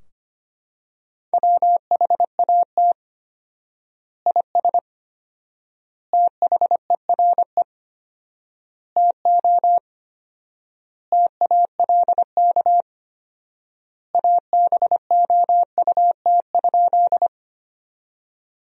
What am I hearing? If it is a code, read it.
Morse code, heard as a keyed tone, about 25 wpm, 700 Hz. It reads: WHAT IS THERE TO TALK ABOUT?
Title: 4x Word-Spacing - Sentences from Top 400 Words - 25wpm